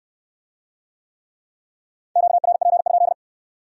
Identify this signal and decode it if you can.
Morse code, heard as a keyed tone, about 56 wpm, 700 Hz. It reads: QRP?
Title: Speed-Racing Q-Codes - 45wpm